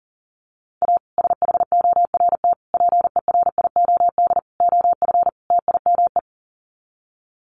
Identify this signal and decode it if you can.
Morse code, heard as a keyed tone, about 40 wpm, 700 Hz. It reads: A SHORT PERIOD OF TIME
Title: Sets of 5 Words Encore 40wpm